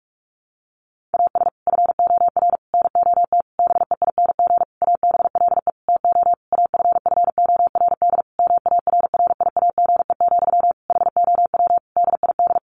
Morse code, heard as a keyed tone, about 44 wpm, 700 Hz. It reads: AS FOR NOT BEING ABLE TO AFFORD MARRIAGE, HOW DID
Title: Sentences from Top 1100 Words 45wpm